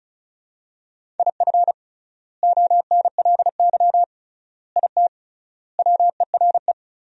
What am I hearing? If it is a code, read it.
Morse code, heard as a keyed tone, about 36 wpm, 700 Hz. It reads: IF ONLY IT WERE
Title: 3x Word-Spacing - Sentences from Top 500 Words - 35wpm